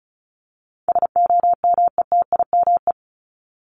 Morse code, heard as a keyed tone, about 34 wpm, 700 Hz. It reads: SOMETIME
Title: ES:S Suffix - Top 100 Words with Base Words 35wpm